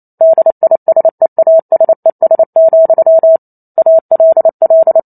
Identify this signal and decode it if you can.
Morse code, heard as a keyed tone, about 28 wpm, 650 Hz. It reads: DISEASES, ALL